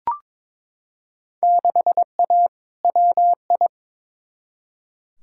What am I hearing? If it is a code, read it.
Morse code, heard as a keyed tone, about 22 wpm, 700 Hz. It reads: 6A WI
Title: Field Day Exchanges - 22wpm